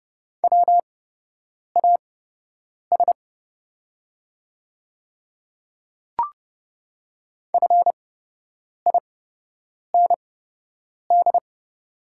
Morse code, heard as a keyed tone, about 30 wpm, spaced as for 8 wpm, 700 Hz. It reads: WAS FIND
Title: 8x Character Spacing - Top 100 Words - 30wpm